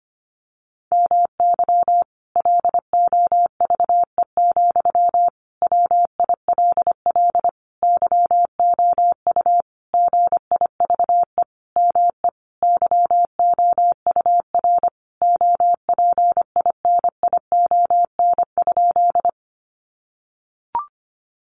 Morse code, heard as a keyed tone, about 25 wpm, 700 Hz. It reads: MY LOVE, WILL YOU GIVE ME YOUR OPINION?
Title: Sentences from Top 1200 Words 25wpm